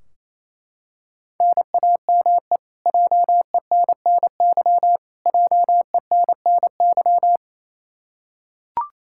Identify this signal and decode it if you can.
Morse code, heard as a keyed tone, about 28 wpm, 700 Hz. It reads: NAME JENNY JENNY
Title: Practice QSO - Exchange 1 - Beginning - 28wpm